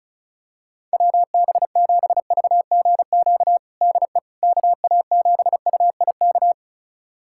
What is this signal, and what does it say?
Morse code, heard as a keyed tone, about 34 wpm, 690 Hz. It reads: WB7VGQ DE KA7UIK
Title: USA Callsign DE Callsign - 6 Characters - 35wpm